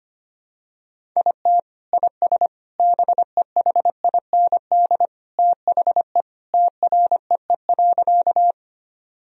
Morse code, heard as a keyed tone, about 25 wpm, 700 Hz. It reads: IT IS BEHIND THE TREE.
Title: Sentences from Top 1000 Words - Review Part 1 - 25wpm